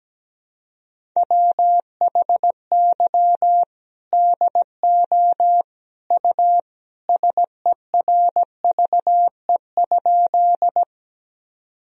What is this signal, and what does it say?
Morse code, heard as a keyed tone, about 17 wpm, 700 Hz. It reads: WHY DO U SERVE?
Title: Sentences from Top 400 Words and Abbreviations 17wpm